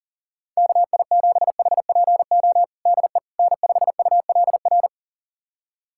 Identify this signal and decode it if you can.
Morse code, heard as a keyed tone, about 40 wpm, 700 Hz. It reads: KI7HPO DE N5ULR
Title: USA Callsign DE Callsign 40wpm